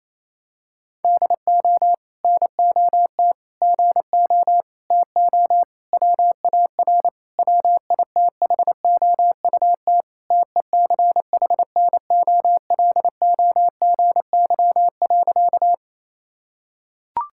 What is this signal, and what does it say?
Morse code, heard as a keyed tone, about 28 wpm, 700 Hz. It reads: DO NOT GO TO WAR WITHOUT TECHNOLOGY.
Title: Sentences from Top 1000 Words - Review Part 5 - 28wpm